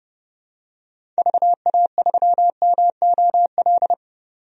Morse code, heard as a keyed tone, about 30 wpm, 700 Hz. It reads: VA3MOL
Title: DXCC-100 Call Signs - 6 Characters - 30wpm